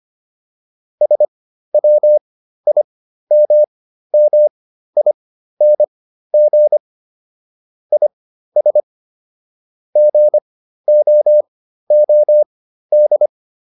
Morse code, heard as a keyed tone, about 25 wpm, spaced as for 13 wpm, 600 Hz. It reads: SWIMMING IS GOOD